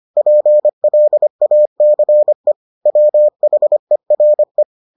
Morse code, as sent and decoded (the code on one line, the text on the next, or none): .--. .-.. .- -.-. / .-- .... .-. .
PLACE WHERE